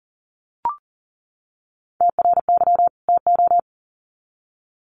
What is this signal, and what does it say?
Morse code, heard as a keyed tone, about 40 wpm, 700 Hz. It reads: TRY TO
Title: Sets of 2 Words 40wpm